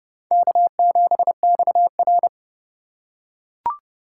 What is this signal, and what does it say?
Morse code, heard as a keyed tone, about 28 wpm, 700 Hz. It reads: K7XR